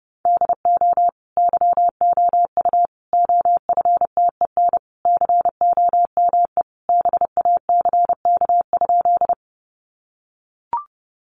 Morse code, heard as a keyed tone, about 30 wpm, 700 Hz. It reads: DO YOU OFTEN COME BACK?
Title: Sentences from Top 300 Words 30wpm